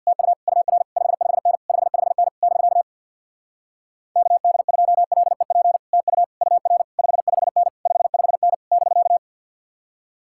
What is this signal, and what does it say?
Morse code, heard as a keyed tone, about 66 wpm, 700 Hz. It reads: TU UR 55N 55N <BK> KD2LEP TU UR 55N 55N <BK>